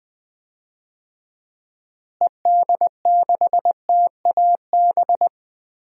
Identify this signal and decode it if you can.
Morse code, heard as a keyed tone, about 20 wpm, 700 Hz. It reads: ED6TAB
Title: DXCC-100 Call Signs - 6 Characters - Repeated 3x - 20wpm